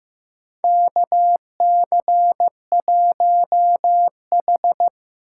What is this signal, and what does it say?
Morse code, heard as a keyed tone, about 15 wpm, 700 Hz. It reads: KC1H
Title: USA Call Signs - 4 Characters - 6x Character Spacing - 15wpm